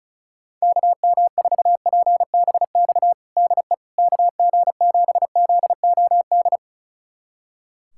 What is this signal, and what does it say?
Morse code, heard as a keyed tone, about 36 wpm, 700 Hz. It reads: KM4PBX DE KG7ZOD
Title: USA Callsign DE Callsign - 6 Characters - 35wpm